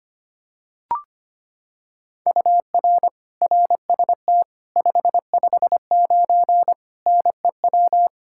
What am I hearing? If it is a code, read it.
Morse code, heard as a keyed tone, about 25 wpm, 700 Hz. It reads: UR RST 559 NEW
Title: Practice QSO - Exchange 1 - Beginning - 25wpm